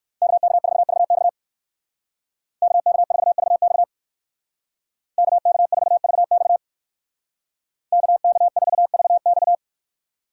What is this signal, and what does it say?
Morse code, heard as a keyed tone, about 56 wpm, 700 Hz. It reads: KK4VX KK4VX KK4VX KK4VX
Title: Speed Racing - USA 2x2 Call Signs 45wpm